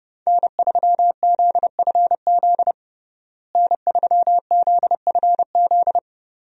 Morse code, heard as a keyed tone, about 30 wpm, 700 Hz. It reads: N3ZFZ N3ZFZ
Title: USA Call Signs - 5 Characters - Repeated 2x - 30wpm